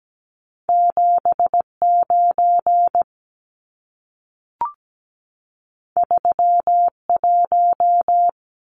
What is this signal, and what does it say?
Morse code, heard as a keyed tone, about 17 wpm, 700 Hz. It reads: 79 31
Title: Numbers - 2-Digit 17wpm